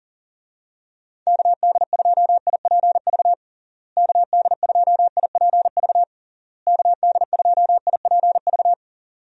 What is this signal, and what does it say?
Morse code, heard as a keyed tone, about 40 wpm, 700 Hz. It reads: KD2IPV KD2IPV KD2IPV
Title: USA Call Signs - 6 Characters - Repeated 3x - 40wpm